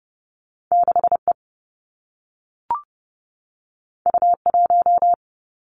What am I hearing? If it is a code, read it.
Morse code, heard as a keyed tone, about 30 wpm, 700 Hz. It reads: 6 E U 1
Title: Two Challenging Letter-Number Combinations 30wpm